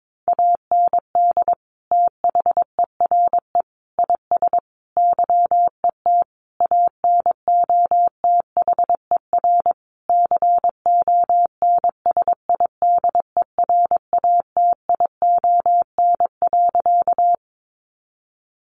Morse code, heard as a keyed tone, about 22 wpm, 700 Hz. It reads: AND THERE IS YET ANOTHER CONSIDERATION.
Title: Sentences from Top 1500 Words 22wpm